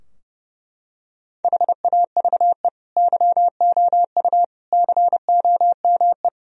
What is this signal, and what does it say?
Morse code, heard as a keyed tone, about 30 wpm, 700 Hz. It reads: HAVE YOU COME
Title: Sentences from Top 1000 Words - Review Part 3 - 30wpm